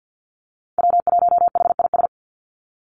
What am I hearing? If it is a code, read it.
Morse code, heard as a keyed tone, about 50 wpm, 700 Hz. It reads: W1HIS
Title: Call Signs - 50wpm